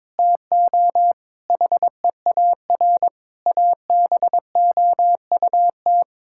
Morse code, heard as a keyed tone, about 22 wpm, 700 Hz. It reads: TO HEAR ABOUT
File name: practice / 22wpm / Sets of 6 Words - 22wpm